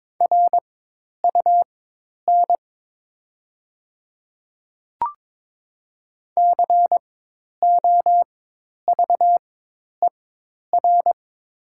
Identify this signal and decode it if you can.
Morse code, heard as a keyed tone, about 22 wpm, spaced as for 10 wpm, 700 Hz. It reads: RUN COVER
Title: 4x Character Spacing - Top 300 Words - 22wpm